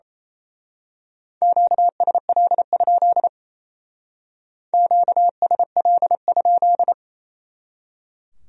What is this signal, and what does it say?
Morse code, heard as a keyed tone, about 29 wpm, 700 Hz. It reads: QSL? QSL?
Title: Speed-Racing Q-Codes - 28wpm